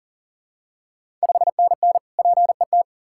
Morse code, heard as a keyed tone, about 40 wpm, 700 Hz. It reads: 5NN PET
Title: Ontario QSO Party Exchange 40wpm